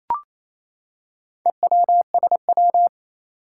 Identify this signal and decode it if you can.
Morse code, heard as a keyed tone, about 29 wpm, 700 Hz. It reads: EWSW
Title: Four Challenging Letter-Number Combinations 28wpm